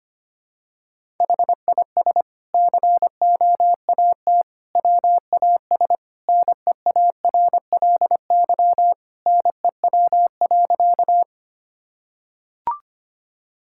Morse code, heard as a keyed tone, about 25 wpm, 700 Hz. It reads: HIS COAT WAS NEARLY NEW.
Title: Sentences from Top 1600 Words 25wpm